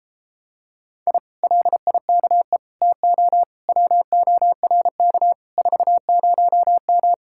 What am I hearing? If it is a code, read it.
Morse code, heard as a keyed tone, about 33 wpm, 700 Hz. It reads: I LIKE TO WORK 40M